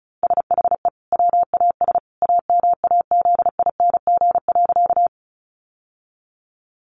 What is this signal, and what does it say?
Morse code, heard as a keyed tone, about 34 wpm, 700 Hz. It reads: SHE WAS AMAZING.
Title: Sentences from Top 1500 Words 35wpm